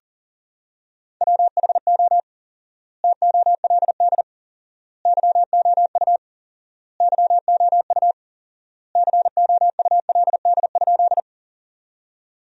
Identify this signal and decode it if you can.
Morse code, heard as a keyed tone, about 40 wpm, 700 Hz. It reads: WHO TOLD YOU YOU COULD?
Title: 4x Word-Spacing - Sentences from Top 400 Words - 40wpm